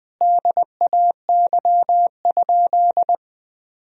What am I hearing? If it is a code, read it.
Morse code, heard as a keyed tone, about 20 wpm, 700 Hz. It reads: DAY?
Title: Sentences from Top 1000 Words - Review Part 4 - 20wpm